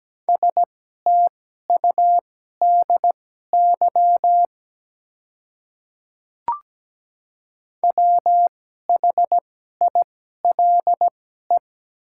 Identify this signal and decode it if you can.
Morse code, heard as a keyed tone, about 17 wpm, spaced as for 12 wpm, 700 Hz. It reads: STUDY WHILE
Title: 2x Character Spacing - Top 300 Words - 17wpm